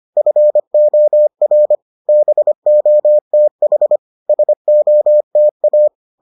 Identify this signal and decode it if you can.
Morse code, heard as a keyed tone, about 25 wpm, 600 Hz. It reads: FOR BOTH SOTA